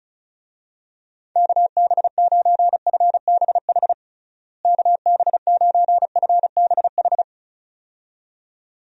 Morse code, heard as a keyed tone, about 36 wpm, 700 Hz. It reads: KB9FBH KB9FBH
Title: USA Call Signs - 6 Characters - Repeated 2x - 35wpm